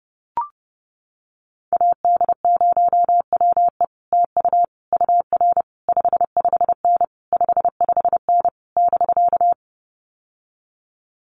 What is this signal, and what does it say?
Morse code, heard as a keyed tone, about 28 wpm, 700 Hz. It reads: AD0WE TU UR 55N 55N <BK>
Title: Common QSO Phrases Encore 30wpm